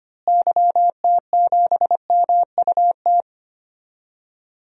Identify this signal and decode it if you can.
Morse code, heard as a keyed tone, about 25 wpm, 700 Hz. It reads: YT7MUT
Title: DXCC-100 Call Signs - 6 Characters - Repeated 3x - 25wpm